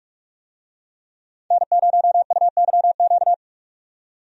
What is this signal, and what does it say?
Morse code, heard as a keyed tone, about 45 wpm, 700 Hz. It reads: N0UYQ
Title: USA Call Signs - 5 Characters - 45wpm